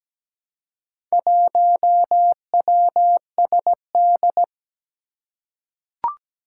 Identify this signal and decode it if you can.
Morse code, heard as a keyed tone, about 17 wpm, 700 Hz. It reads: 1WSD
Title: Four Challenging Letter-Number Combinations 17wpm